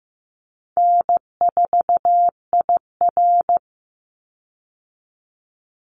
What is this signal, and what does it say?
Morse code, heard as a keyed tone, about 15 wpm, 700 Hz. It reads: N4IR
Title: Easy Call Signs Repeated 2x - 15wpm